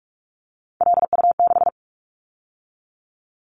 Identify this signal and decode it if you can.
Morse code, heard as a keyed tone, about 46 wpm, 700 Hz. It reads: LU6
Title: Callsign Prefixes for Top 100 DXCC - 45wpm